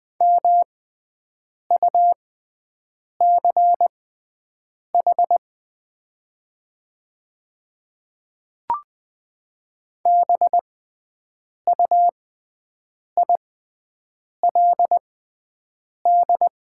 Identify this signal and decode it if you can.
Morse code, heard as a keyed tone, about 20 wpm, spaced as for 7 wpm, 700 Hz. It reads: MUCH BUILD
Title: 6x Character Spacing - Top 200 Words - 20wpm